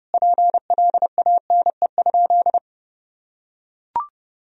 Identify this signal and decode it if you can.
Morse code, heard as a keyed tone, about 30 wpm, 700 Hz. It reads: PLANE?
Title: Sentences from Top 1000 Words - Review Part 3 - 30wpm